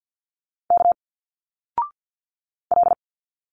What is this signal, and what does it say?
Morse code, heard as a keyed tone, about 49 wpm, 700 Hz. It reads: K L
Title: Single Letters 50wpm